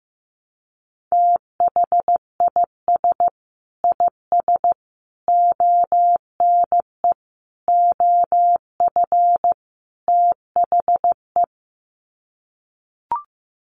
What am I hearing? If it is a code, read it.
Morse code, heard as a keyed tone, about 15 wpm, 700 Hz. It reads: THIS IS ONE OF THE
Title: Sets of 5 Words 15wpm